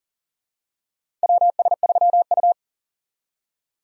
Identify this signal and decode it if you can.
Morse code, heard as a keyed tone, about 40 wpm, 700 Hz. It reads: WS3U
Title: USA Call Signs - 4 Characters - 40wpm